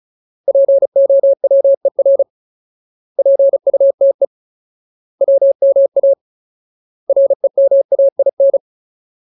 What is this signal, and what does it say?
Morse code, heard as a keyed tone, about 37 wpm, 550 Hz. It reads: POWER PUTE WMA REMAIN
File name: Ultimate Word List / 35WPM Ultimate